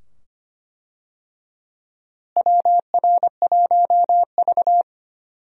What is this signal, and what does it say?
Morse code, heard as a keyed tone, about 25 wpm, 700 Hz. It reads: WR1V
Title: USA Call Signs - 4 Characters - Repeated 2x - 25wpm